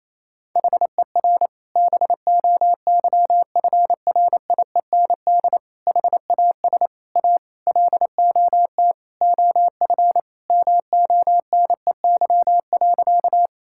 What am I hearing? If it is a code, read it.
Morse code, heard as a keyed tone, about 28 wpm, 700 Hz. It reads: HER BOYFRIEND HAS A LOT OF MONEY.
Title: Sentences from Top 1000 Words - Review Part 5 - 28wpm